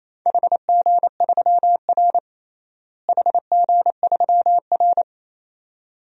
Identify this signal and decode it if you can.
Morse code, heard as a keyed tone, about 28 wpm, 700 Hz. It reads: HG3R HG3R
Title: DXCC-100 Call Signs - 4 Characters - Repeated 2x - 28wpm